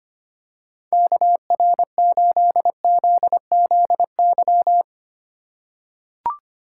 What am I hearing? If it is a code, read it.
Morse code, heard as a keyed tone, about 25 wpm, 700 Hz. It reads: KR8ZZY
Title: USA Call Signs - 6 Characters - 2x Character Spacing - 25wpm